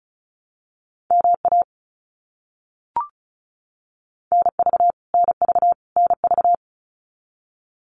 Morse code, heard as a keyed tone, about 34 wpm, 700 Hz. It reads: MA NV NV NV
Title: US States Repeated 3x 35wpm